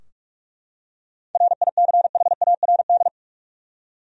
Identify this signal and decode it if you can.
Morse code, heard as a keyed tone, about 41 wpm, 700 Hz. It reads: RICHARD
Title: USA Common Names - 45wpm